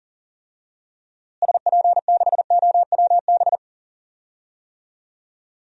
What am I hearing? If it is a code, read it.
Morse code, heard as a keyed tone, about 40 wpm, 700 Hz. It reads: SP6OWB
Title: DXCC-100 Call Signs - 6 Characters - Repeated 2x - 40wpm